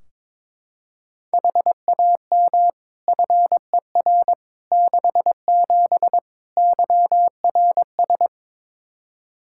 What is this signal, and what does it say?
Morse code, heard as a keyed tone, about 22 wpm, 700 Hz. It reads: HAM FER 67 YRS
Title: Practice QSO - Exchange 3 - More Info - 22wpm